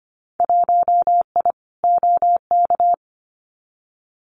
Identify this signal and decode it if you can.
Morse code, heard as a keyed tone, about 25 wpm, 700 Hz. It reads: 1I OK
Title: Winter Field Day Exchanges Encore 25wpm